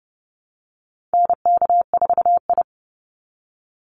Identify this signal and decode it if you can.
Morse code, heard as a keyed tone, about 30 wpm, 700 Hz. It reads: NK4I
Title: Call Signs Repeated 2x - 30wpm